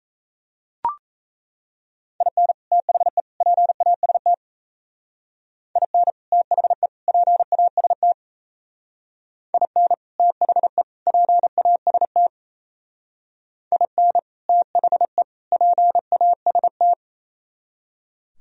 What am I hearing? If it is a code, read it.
Morse code, heard as a keyed tone, about 34 wpm, 700 Hz. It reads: IN THE PAST IN THE PAST IN THE PAST IN THE PAST
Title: Speed-Racing - Sets of 3 Words 28wpm